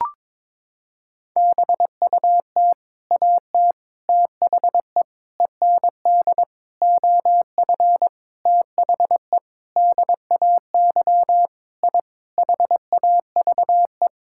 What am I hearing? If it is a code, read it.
Morse code, heard as a keyed tone, about 22 wpm, 700 Hz. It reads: BUT AT THE END OF THE DAY I HAVE
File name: practice / 22wpm / Sentences with 8-Word Phrases - 22wpm